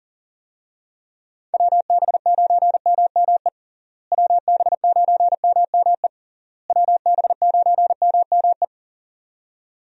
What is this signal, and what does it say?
Morse code, heard as a keyed tone, about 40 wpm, 700 Hz. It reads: WB9MME WB9MME WB9MME
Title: USA Call Signs - 6 Characters - Repeated 3x - 40wpm